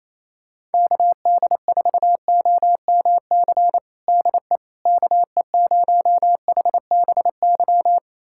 Morse code, heard as a keyed tone, about 28 wpm, 700 Hz. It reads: KD4OMC DE KE0HBY
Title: USA Callsign DE Callsign - 6 Characters - 28wpm